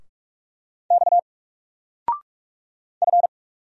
Morse code, heard as a keyed tone, about 46 wpm, 700 Hz. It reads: X F
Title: Single Letter-Number 45wpm